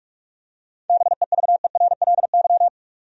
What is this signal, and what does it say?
Morse code, heard as a keyed tone, about 46 wpm, 700 Hz. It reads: BEVERLY
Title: USA Common Names - 45wpm